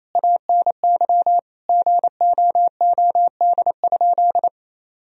Morse code, heard as a keyed tone, about 28 wpm, 700 Hz. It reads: ANY GOOD?
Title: Sentences from Top 1000 Words - Review Part 4 - 28wpm